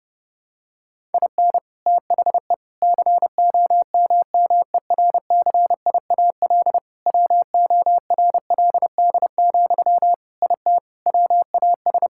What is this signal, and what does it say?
Morse code, heard as a keyed tone, about 32 wpm, 700 Hz. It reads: IN THE COMMERCIAL WORLD, IT WAS
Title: Sentences from Top 1000 Words - Review Part 5 - 30wpm